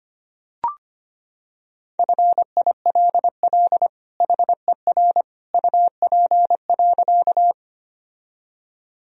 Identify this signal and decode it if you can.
Morse code, heard as a keyed tone, about 25 wpm, 700 Hz. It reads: FILL HER UP.
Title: Sentences from Top 1000 Words - Review Part 3 - 25wpm